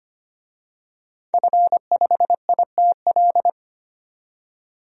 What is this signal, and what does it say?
Morse code, heard as a keyed tone, about 25 wpm, 690 Hz. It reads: F5ITL
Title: DXCC-100 Call Signs - 5 Characters - 25wpm